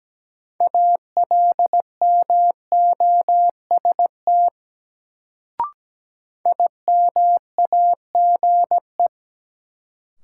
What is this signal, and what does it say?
Morse code, heard as a keyed tone, about 17 wpm, 700 Hz. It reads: ALMOST IMAGE
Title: Sentences from Top 700 Words - New Word Drill - 17wpm